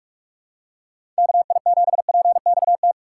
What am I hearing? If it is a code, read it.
Morse code, heard as a keyed tone, about 42 wpm, 700 Hz. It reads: KI7PXT
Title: USA Call Signs - 6 Characters - 2x Character Spacing - 45wpm